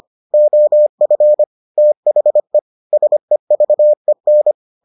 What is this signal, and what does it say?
Morse code, heard as a keyed tone, about 25 wpm, 600 Hz. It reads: OF THE SEVEN